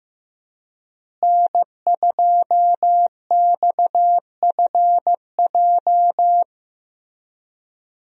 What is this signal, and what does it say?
Morse code, heard as a keyed tone, about 15 wpm, 700 Hz. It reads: N2XFJ
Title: USA Call Signs - 5 Characters - Repeated 3x - 15wpm